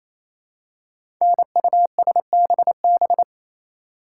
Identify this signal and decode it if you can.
Morse code, heard as a keyed tone, about 29 wpm, 700 Hz. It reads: NUSBB
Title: Code Groups - 5 Letters - 28wpm